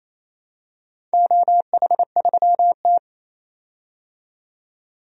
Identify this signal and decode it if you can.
Morse code, heard as a keyed tone, about 28 wpm, 700 Hz. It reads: OH3T
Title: DXCC-100 Call Signs - 4 Characters - 28wpm